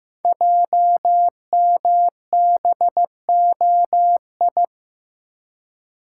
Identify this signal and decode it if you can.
Morse code, heard as a keyed tone, about 15 wpm, 700 Hz. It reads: JMBOI
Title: Code Groups - 5 Letters - 15wpm